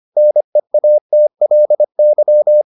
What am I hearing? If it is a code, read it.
Morse code, heard as a keyed tone, about 25 wpm, 600 Hz. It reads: NEATLY